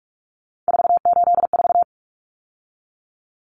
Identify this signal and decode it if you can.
Morse code, heard as a keyed tone, about 44 wpm, 700 Hz. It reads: 484
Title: Numbers - 3-Digit 45wpm